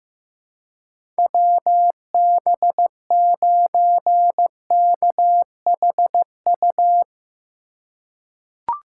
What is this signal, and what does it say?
Morse code, heard as a keyed tone, about 15 wpm, 700 Hz. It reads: WB9KHU
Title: USA Call Signs - 6 Characters - 3x Character Spacing - 15wpm